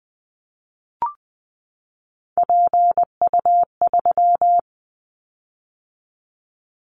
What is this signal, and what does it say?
Morse code, heard as a keyed tone, about 19 wpm, 700 Hz. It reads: PU3
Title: Callsign Prefixes for Top 50 DXCC - 20wpm